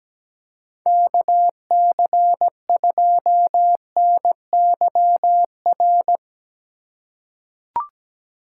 Morse code, heard as a keyed tone, about 17 wpm, 700 Hz. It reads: KC2NYR
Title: USA Call Signs - 6 Characters - 3x Character Spacing - 17wpm